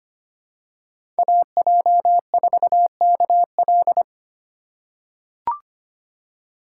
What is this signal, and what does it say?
Morse code, heard as a keyed tone, about 25 wpm, 700 Hz. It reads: AJ4KL
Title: USA Call Signs - 3x Character Spacing - 25wpm